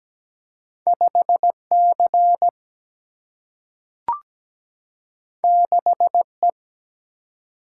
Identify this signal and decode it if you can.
Morse code, heard as a keyed tone, about 17 wpm, 700 Hz. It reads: 5 C 6 E
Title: Single-Letter Single-Number 17wpm